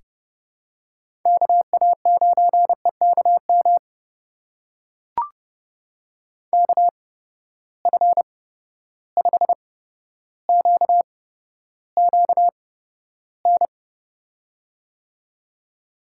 Morse code, heard as a keyed tone, about 30 wpm, spaced as for 13 wpm, 700 Hz. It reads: KA9EKM KF5QQN
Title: USA Call Signs - 6 Characters - 8x Character Spacing - 30wpm